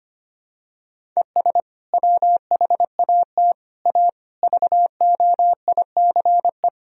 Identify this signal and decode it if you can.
Morse code, heard as a keyed tone, about 25 wpm, 700 Hz. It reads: ES WHAT A VOICE
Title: Sentences from Top 500 Words and Abbreviations 25wpm